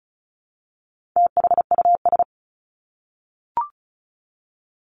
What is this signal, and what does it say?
Morse code, heard as a keyed tone, about 34 wpm, 700 Hz. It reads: THUS
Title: Top 751-1000 Words 35wpm